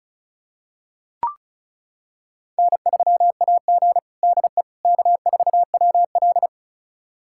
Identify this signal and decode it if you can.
Morse code, heard as a keyed tone, about 34 wpm, 700 Hz. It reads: N3AG DE K4WL
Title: USA Callsign DE Callsign - 4 Characters - 35wpm